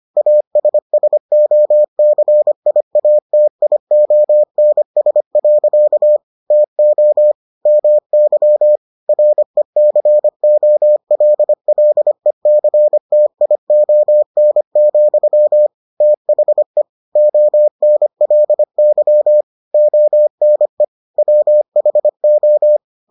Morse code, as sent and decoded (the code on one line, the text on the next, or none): .- ... ... --- -.-. .. .- - .. --- -. ... .-.-.- / - --- / -- -.-- / .-. -.-. --- .-.. .-.. -.-. - .. --- -. --..-- / - .... / --- -. .-.. -.-- / --- -. / .-- .... ---
ASSOCIATIONS. TO MY RECOLLECTION, THE ONLY ONE WHO